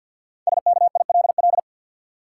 Morse code, heard as a keyed tone, about 52 wpm, 700 Hz. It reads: SKILL
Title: Sentences from Top 1200 Words - New Word Drill - 50wpm